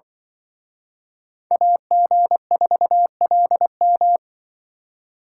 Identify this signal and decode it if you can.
Morse code, heard as a keyed tone, about 25 wpm, 700 Hz. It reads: AG4LM